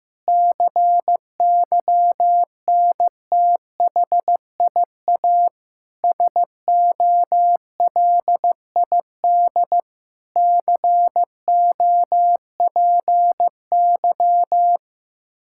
Morse code, heard as a keyed tone, about 15 wpm, 700 Hz. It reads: CYNTHIA SOLID COPY
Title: K1USN SST QSO Practice 15wpm